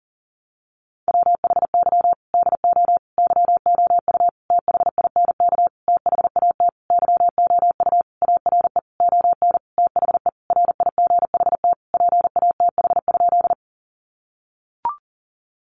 Morse code, heard as a keyed tone, about 40 wpm, 700 Hz. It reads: WHY DO YOU THINK THAT YOU ARE ON THE RIGHT PATH?
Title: Sentences from Top 1400 Words 40wpm